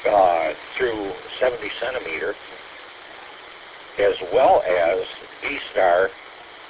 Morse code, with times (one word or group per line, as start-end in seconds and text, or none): none